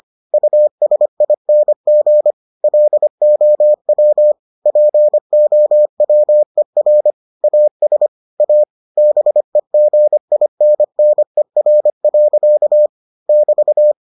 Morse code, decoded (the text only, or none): USING LOW POWER AS A BEGINNER. <BT>